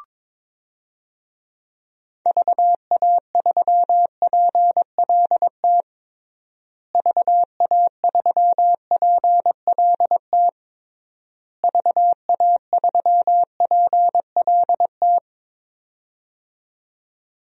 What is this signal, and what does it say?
Morse code, heard as a keyed tone, about 22 wpm, 700 Hz. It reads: VA3PLT VA3PLT VA3PLT
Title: DXCC-100 Call Signs - 6 Characters - Repeated 3x - 22wpm